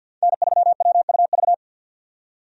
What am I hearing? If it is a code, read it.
Morse code, heard as a keyed tone, about 53 wpm, 700 Hz. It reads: N3WUV
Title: USA Call Signs - 5 Characters - 50wpm